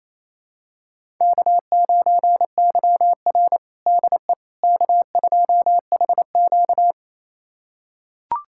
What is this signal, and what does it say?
Morse code, heard as a keyed tone, about 28 wpm, 700 Hz. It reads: K9YR DE K2HQ
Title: USA Callsign DE Callsign - 4 Characters - 28wpm